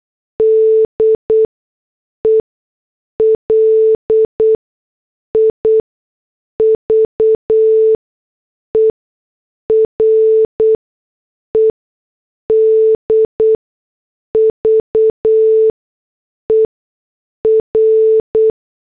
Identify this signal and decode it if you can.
Morse code, heard as a keyed tone, about 8 wpm, spaced as for 6 wpm, 440 Hz. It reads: DELIVERED VER